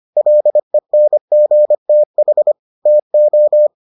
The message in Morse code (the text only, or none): LENGTH TO